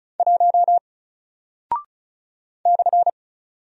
Morse code, heard as a keyed tone, about 36 wpm, 700 Hz. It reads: 1 /
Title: Single Character (Letter:Number:Punctuation:Prosign) Alt 35wpm